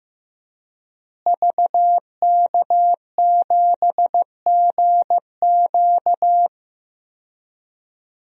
Morse code, heard as a keyed tone, about 15 wpm, 700 Hz. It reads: VK7GQ
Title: DXCC-100 Call Signs - 5 Characters - Repeated 2x - 15wpm